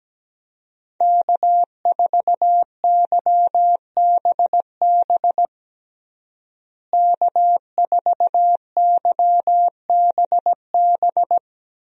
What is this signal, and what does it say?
Morse code, heard as a keyed tone, about 17 wpm, 700 Hz. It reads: K4YBB K4YBB
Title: USA Call Signs - 5 Characters - Repeated 2x - 17wpm